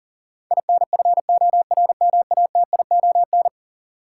Morse code, heard as a keyed tone, about 40 wpm, 700 Hz. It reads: INFORMATION
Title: Sentences from Top 600 Words - New Word Drill - 40wpm